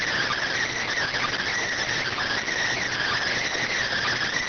Emergency Alert System (SAME) tones, heard as harsh buzzing bursts. It is heard from 0.7 to 3.7 s.